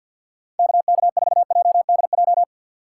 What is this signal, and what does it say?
Morse code, heard as a keyed tone, about 47 wpm, 700 Hz. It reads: KK4JDJ
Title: USA Call Signs - 4x Character Spacing - 50wpm